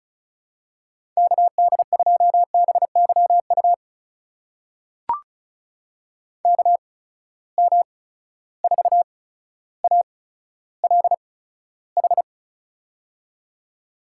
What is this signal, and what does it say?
Morse code, heard as a keyed tone, about 36 wpm, spaced as for 15 wpm, 700 Hz. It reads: KD2BYU KM4ALH
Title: USA Call Signs - 6 Characters - 8x Character Spacing - 35wpm